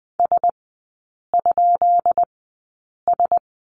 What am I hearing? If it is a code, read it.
Morse code, heard as a keyed tone, about 20 wpm, spaced as for 8 wpm, 700 Hz. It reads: S?S
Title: 184 Lesson 30e - Letter b - Three Characters 20wpm